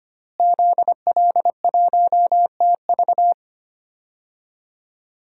Morse code, heard as a keyed tone, about 25 wpm, 700 Hz. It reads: ZL1TV
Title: DXCC-100 Call Signs - 5 Characters - Repeated 3x - 25wpm